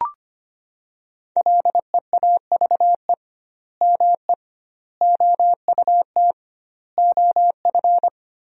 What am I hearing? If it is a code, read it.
Morse code, heard as a keyed tone, about 25 wpm, 700 Hz. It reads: LEAVE ME OUT OF
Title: 2x Word-Spacing - Sentences from Top 400 Words - 25wpm